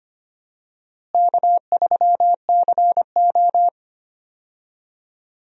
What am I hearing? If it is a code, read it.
Morse code, heard as a keyed tone, about 25 wpm, 700 Hz. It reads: K3CO
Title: USA Call Signs - 4 Characters - Repeated 2x - 25wpm